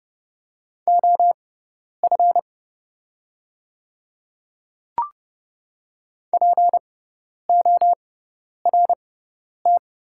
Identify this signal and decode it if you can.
Morse code, heard as a keyed tone, about 30 wpm, spaced as for 10 wpm, 700 Hz. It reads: OF PORT
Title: 6x Character Spacing - Top 200 Words - 30wpm